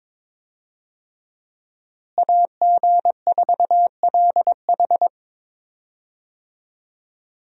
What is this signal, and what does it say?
Morse code, heard as a keyed tone, about 22 wpm, 700 Hz. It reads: AG4LH
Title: USA Call Signs - 5 Characters - Repeated 3x - 22wpm